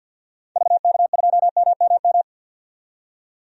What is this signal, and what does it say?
Morse code, heard as a keyed tone, about 52 wpm, 700 Hz. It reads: VK2MMM
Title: DXCC-100 Call Signs - 6 Characters - 50wpm